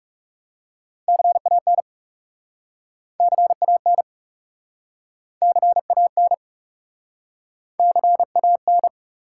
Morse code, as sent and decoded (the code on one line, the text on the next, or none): -.-. .- -. / -.-. .- -. / -.-. .- -. / -.-. .- -.
CAN CAN CAN CAN